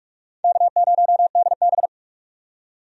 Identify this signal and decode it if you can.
Morse code, heard as a keyed tone, about 42 wpm, 700 Hz. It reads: K0DB
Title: USA Call Signs - 4 Characters - 45wpm